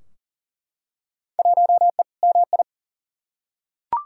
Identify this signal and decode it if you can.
Morse code, heard as a keyed tone, about 37 wpm, 700 Hz. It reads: 1E MI
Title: Field Day Exchanges - 40wpm